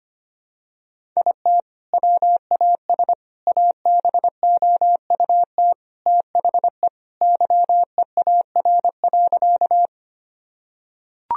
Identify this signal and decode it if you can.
Morse code, heard as a keyed tone, about 25 wpm, 700 Hz. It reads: IT WAS ABOUT THE YEAR.
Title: Sentences from Top 1000 Words - Review Part 1 - 25wpm